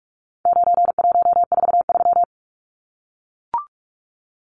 Morse code, heard as a keyed tone, about 44 wpm, 700 Hz. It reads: 9143
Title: Numbers - 4-Digit 45wpm